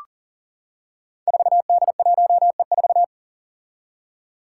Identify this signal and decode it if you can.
Morse code, heard as a keyed tone, about 37 wpm, 700 Hz. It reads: 4D1E4
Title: Five Challenging Letter-Number Combinations 40wpm